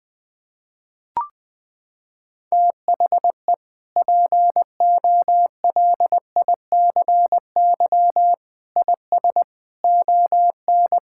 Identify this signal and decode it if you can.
Morse code, heard as a keyed tone, about 20 wpm, 700 Hz. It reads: THE POLICY IS ON
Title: Sentences from Top 1000 Words - Review Part 5 - 20wpm